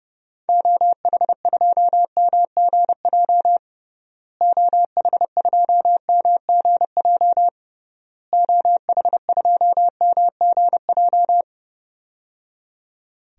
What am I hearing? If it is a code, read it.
Morse code, heard as a keyed tone, about 29 wpm, 700 Hz. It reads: OH2MGJ OH2MGJ OH2MGJ
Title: DXCC-100 Call Signs - 6 Characters - Repeated 3x - 30wpm